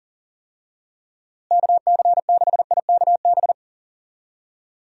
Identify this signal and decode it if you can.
Morse code, heard as a keyed tone, about 40 wpm, 700 Hz. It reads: KC6IKB
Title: USA Call Signs - 6 Characters - Repeated 3x - 40wpm